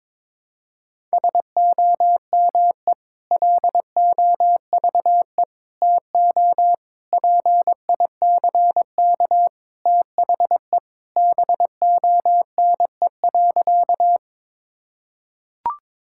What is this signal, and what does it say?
Morse code, heard as a keyed tone, about 22 wpm, 700 Hz. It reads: SOME LOVE TO PICK THE BONE.
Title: Sentences from Top 1600 Words 22wpm